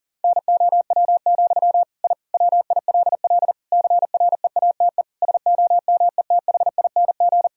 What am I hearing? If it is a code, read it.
Morse code, heard as a keyed tone, about 37 wpm, 700 Hz. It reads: NOW, I WILL CREATE SOMETHING
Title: Sentences from Top 1000 Words - Review Part 3 - 40wpm